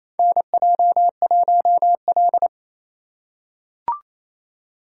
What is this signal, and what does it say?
Morse code, heard as a keyed tone, about 27 wpm, 690 Hz. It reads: NJ1L
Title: USA Call Signs - 4 Characters - 2x Character Spacing - 28wpm